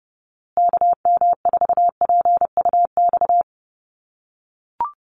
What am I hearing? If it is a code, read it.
Morse code, heard as a keyed tone, about 30 wpm, 700 Hz. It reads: KM4PUX
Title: USA Call Signs Repeated 3x 30wpm